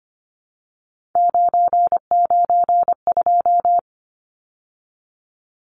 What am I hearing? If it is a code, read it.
Morse code, heard as a keyed tone, about 24 wpm, 690 Hz. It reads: 992
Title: Numbers - 3-Digit 25wpm